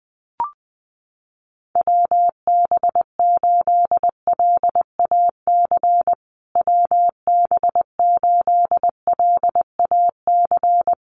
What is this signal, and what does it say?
Morse code, heard as a keyed tone, about 20 wpm, 700 Hz. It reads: WB8LAC WB8LAC
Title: USA Call Signs Repeated 2x 20wpm